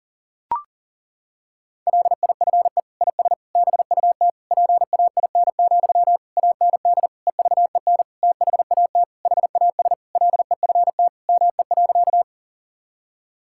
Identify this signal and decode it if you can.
Morse code, heard as a keyed tone, about 40 wpm, 700 Hz. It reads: LIFE IS BUT PAIN, AND EVEN THAT HAS LEFT ME.
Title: Sentences from Top 1600 Words 40wpm